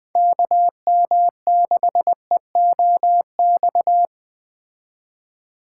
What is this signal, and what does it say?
Morse code, heard as a keyed tone, about 20 wpm, 700 Hz. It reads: KM6EOX
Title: Speed Racing - USA 2x3 Call Signs 17wpm